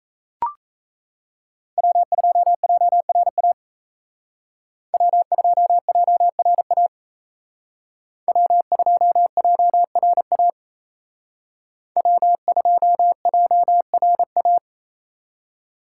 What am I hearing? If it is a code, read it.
Morse code, heard as a keyed tone, about 36 wpm, 700 Hz. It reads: W2JRA W2JRA W2JRA W2JRA